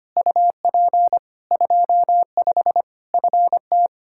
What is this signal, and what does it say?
Morse code, heard as a keyed tone, about 25 wpm, 700 Hz. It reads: UP 25 FT
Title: K1USN SST QSO Practice 25wpm